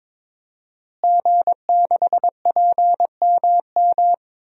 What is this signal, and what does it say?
Morse code, heard as a keyed tone, about 22 wpm, 700 Hz. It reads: G6PMM
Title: DXCC-100 Call Signs - 5 Characters - Repeated 2x - 22wpm